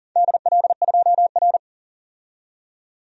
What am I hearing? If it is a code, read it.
Morse code, heard as a keyed tone, about 40 wpm, 700 Hz. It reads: DL2R
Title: DXCC-100 Call Signs - 4 Characters - 40wpm